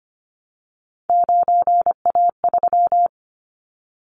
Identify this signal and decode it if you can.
Morse code, heard as a keyed tone, about 25 wpm, 700 Hz. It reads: 9A3
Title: Callsign Prefixes for Top 100 DXCC - 25wpm